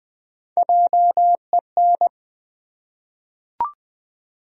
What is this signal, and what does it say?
Morse code, heard as a keyed tone, about 20 wpm, 700 Hz. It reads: JEN
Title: USA Common Names - 20wpm